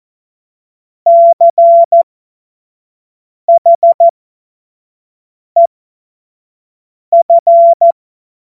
Morse code, heard as a keyed tone, about 14 wpm, spaced as for 5 wpm, 680 Hz. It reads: CHEF
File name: Morsekurs / Witze / Witz 05 Dezember 2023 25 F70